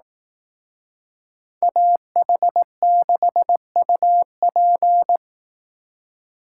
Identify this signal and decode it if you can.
Morse code, heard as a keyed tone, about 18 wpm, 700 Hz. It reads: AH6UP
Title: Speed Racing - USA 2x2 Call Signs 15wpm